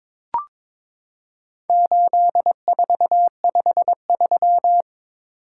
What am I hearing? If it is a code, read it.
Morse code, heard as a keyed tone, about 22 wpm, 700 Hz. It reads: 8453
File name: practice / 22wpm / Four Letter-Number Combinations - 22wpm